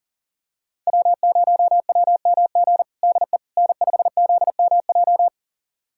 Morse code, heard as a keyed tone, about 40 wpm, 700 Hz. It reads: W0WMG DE N5ZMJ
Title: USA Callsign DE Callsign - 5 Characters - 40wpm